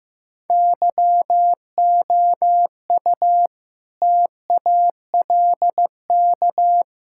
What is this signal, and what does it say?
Morse code, heard as a keyed tone, about 15 wpm, 700 Hz. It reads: YOU TALK
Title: Sentences from Top 1600 Words 15wpm